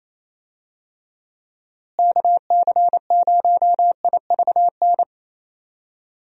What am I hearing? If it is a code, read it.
Morse code, heard as a keyed tone, about 27 wpm, 700 Hz. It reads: KC0IVN
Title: USA Call Signs - 6 Characters - Repeated 2x - 28wpm